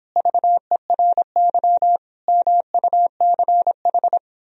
Morse code, heard as a keyed tone, about 26 wpm, 700 Hz. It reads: VERY MUCH
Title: Speed-Racing - Sets of 5 Words 22wpm